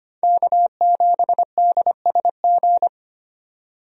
Morse code, heard as a keyed tone, about 25 wpm, 700 Hz. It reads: K7DSG
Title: USA Call Signs - 5 Characters - 25wpm